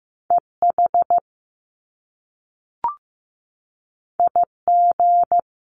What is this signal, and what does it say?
Morse code, heard as a keyed tone, about 15 wpm, 700 Hz. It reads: EH IG
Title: Top Two Letter Combinations in Words 15wpm